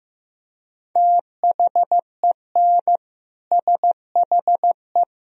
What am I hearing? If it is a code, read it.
Morse code, heard as a keyed tone, about 15 wpm, 700 Hz. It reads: THEN SHE
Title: Sentences from Top 1000 Words - Review Part 2 - 15wpm